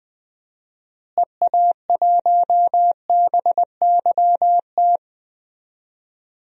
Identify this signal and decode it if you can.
Morse code, heard as a keyed tone, about 19 wpm, 700 Hz. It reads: EA1BYT
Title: DXCC-100 Call Signs - 6 Characters - Repeated 3x - 20wpm